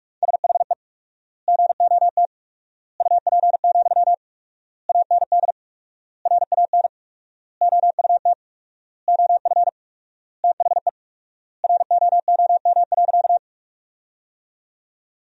Morse code, heard as a keyed tone, about 46 wpm, 700 Hz. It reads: SHE GOT UP, AND RAN OUT OF THE ROOM.
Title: 4x Word-Spacing - Sentences from Top 500 Words - 45wpm